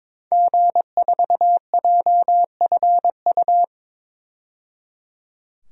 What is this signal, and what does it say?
Morse code, heard as a keyed tone, about 22 wpm, 700 Hz. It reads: G4JFU
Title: DXCC-100 Call Signs - 5 Characters - 22wpm